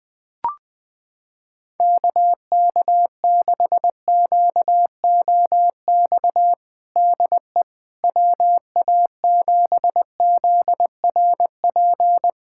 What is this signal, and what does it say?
Morse code, heard as a keyed tone, about 20 wpm, 700 Hz. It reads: KK6QOX DE WA7ZRP
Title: USA Callsign DE Callsign - 6 Characters - 20wpm